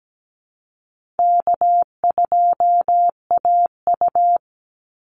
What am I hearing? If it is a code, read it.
Morse code, heard as a keyed tone, about 17 wpm, 700 Hz. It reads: K2AU
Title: Call Signs Repeated 3x - 17wpm